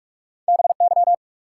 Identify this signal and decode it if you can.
Morse code, heard as a keyed tone, about 47 wpm, 700 Hz. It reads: BY
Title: Sentences from Top 100 Words - New Word Drill - 45wpm